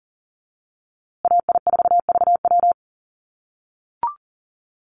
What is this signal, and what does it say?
Morse code, heard as a keyed tone, about 40 wpm, 700 Hz. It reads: AI4VW
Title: USA Call Signs Repeated 3x 40wpm